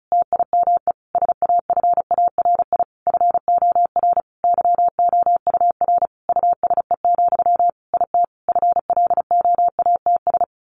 Morse code, heard as a keyed tone, about 34 wpm, 700 Hz. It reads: TIME SAFARI FOR YOUR USE, IT FLOATS
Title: A Sound Of Thunder 35wpm